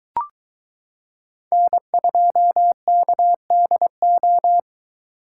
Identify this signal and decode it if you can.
Morse code, heard as a keyed tone, about 22 wpm, 700 Hz. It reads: N2KDO